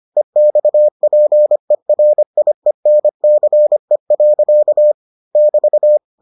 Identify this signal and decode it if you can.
Morse code, heard as a keyed tone, about 25 wpm, 600 Hz. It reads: EXPERIENCE. <BT>